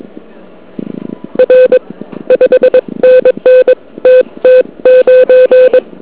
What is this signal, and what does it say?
Morse code, heard as a keyed tone, about 22 wpm, 2550 Hz. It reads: R 5NN TT9
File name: ARCHIV / KX3 / KV PA